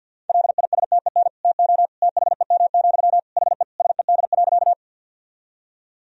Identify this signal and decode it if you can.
Morse code, heard as a keyed tone, about 52 wpm, 700 Hz. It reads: LISTEN TO THEM, HE SED.
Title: Sentences from Top 400 Words and Abbreviations 50wpm